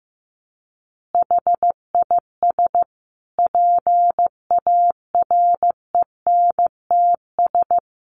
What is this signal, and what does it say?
Morse code, heard as a keyed tone, about 14 wpm, 700 Hz. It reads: HIS PARENTS
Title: Easy Phrases 15wpm